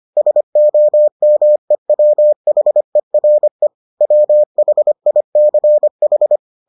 Morse code, heard as a keyed tone, about 25 wpm, 600 Hz. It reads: SOMEWHERE WHICH